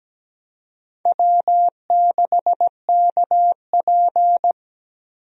Morse code, heard as a keyed tone, about 17 wpm, 700 Hz. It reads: W6KP
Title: USA Call Signs - 4 Characters - 17wpm